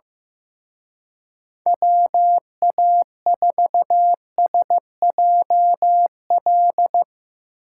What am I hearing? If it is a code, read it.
Morse code, heard as a keyed tone, about 16 wpm, 700 Hz. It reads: WA4SJL
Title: Speed Racing - USA 2x3 Call Signs 15wpm